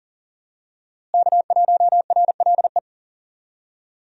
Morse code, heard as a keyed tone, about 39 wpm, 700 Hz. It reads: K1RLE
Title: USA Call Signs - 5 Characters - Repeated 3x - 40wpm